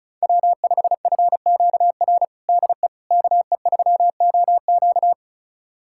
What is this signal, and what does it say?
Morse code, heard as a keyed tone, about 34 wpm, 700 Hz. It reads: W5FQR DE KE3OQ
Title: USA Callsign DE Callsign - 5 Characters - 35wpm